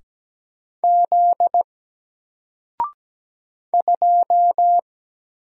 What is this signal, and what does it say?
Morse code, heard as a keyed tone, about 17 wpm, 700 Hz. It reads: Z 2
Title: Single Letter-Number Alt 17wpm